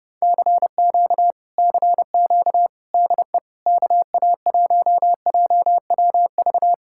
Morse code, heard as a keyed tone, about 31 wpm, 700 Hz. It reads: CQ CQ DE KA1JWV